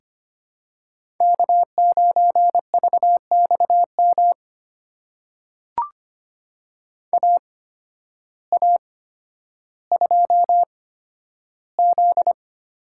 Morse code, heard as a keyed tone, about 25 wpm, spaced as for 12 wpm, 700 Hz. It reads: K9VXM AA2Z
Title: USA Call Signs - 5 Characters - 8x Character Spacing - 25wpm